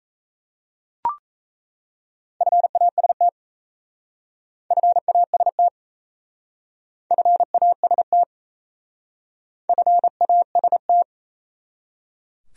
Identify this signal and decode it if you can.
Morse code, heard as a keyed tone, about 35 wpm, 700 Hz. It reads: FAST FAST FAST FAST